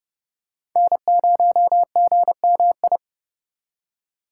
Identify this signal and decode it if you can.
Morse code, heard as a keyed tone, about 31 wpm, 700 Hz. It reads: N0GMI
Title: USA Call Signs - 5 Characters - 3x Character Spacing - 30wpm